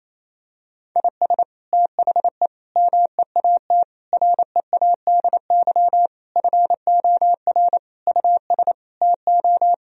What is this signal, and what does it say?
Morse code, heard as a keyed tone, about 28 wpm, 700 Hz. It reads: IS THE MEAT READY FOR US TO
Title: Sentences from Top 1000 Words - Review Part 3 - 28wpm